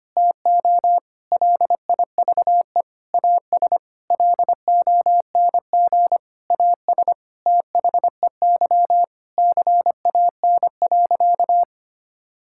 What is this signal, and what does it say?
Morse code, heard as a keyed tone, about 25 wpm, 700 Hz. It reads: TO LIVE AS LONG AS THEY CAN.
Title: Sentences from Top 1000 Words - Review Part 5 - 25wpm